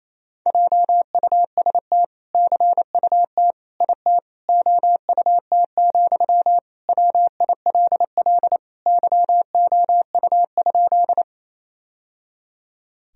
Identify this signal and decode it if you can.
Morse code, heard as a keyed tone, about 28 wpm, 700 Hz. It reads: JUST CUT IT OUT, WILL YOU?
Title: Sentences from Top 1000 Words - Review Part 1 - 28wpm